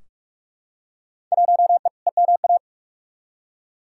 Morse code, heard as a keyed tone, about 46 wpm, 700 Hz. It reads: 1E EMA
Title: Field Day Exchanges Encore - 45wpm